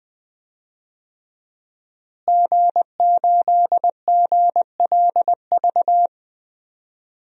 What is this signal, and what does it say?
Morse code, heard as a keyed tone, about 21 wpm, 700 Hz. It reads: G8GLV
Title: DXCC-100 Call Signs - 5 Characters - Repeated 3x - 20wpm